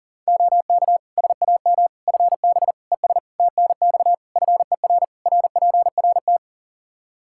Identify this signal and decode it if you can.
Morse code, heard as a keyed tone, about 37 wpm, 700 Hz. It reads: OK SAM FB ES TNX FER RPRT
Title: Practice QSO - Exchange 2 - Info - 40wpm